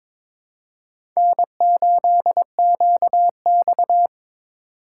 Morse code, heard as a keyed tone, about 22 wpm, 700 Hz. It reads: N8QX
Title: USA Call Signs - 4 Characters - 4x Character Spacing - 22wpm